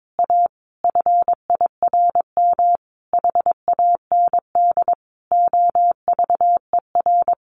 Morse code, heard as a keyed tone, about 22 wpm, 700 Hz. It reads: A FIRM HAND OVER
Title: Sentences from Top 1400 Words 22wpm